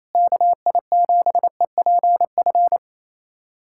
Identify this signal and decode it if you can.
Morse code, heard as a keyed tone, about 28 wpm, 700 Hz. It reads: KI7EPF
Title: USA Callsign DE Callsign - 6 Characters - 28wpm